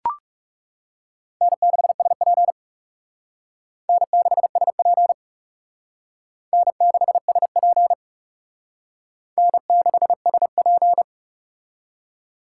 Morse code, heard as a keyed tone, about 40 wpm, 700 Hz. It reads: N6SP N6SP N6SP N6SP